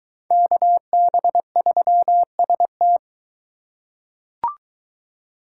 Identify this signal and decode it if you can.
Morse code, heard as a keyed tone, about 22 wpm, 700 Hz. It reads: KB3ST